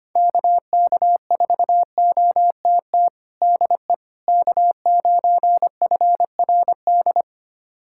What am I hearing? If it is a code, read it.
Morse code, heard as a keyed tone, about 25 wpm, 700 Hz. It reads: KK4OTT DE K9FRD
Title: USA Callsign DE Callsign 25wpm